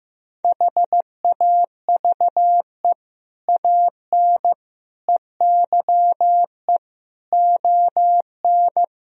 Morse code, heard as a keyed tone, about 16 wpm, 700 Hz. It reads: HAVE AN EYE ON
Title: Sentences from Top 1000 Words - Review Part 1 - 15wpm